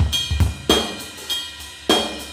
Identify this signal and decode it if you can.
There is no code.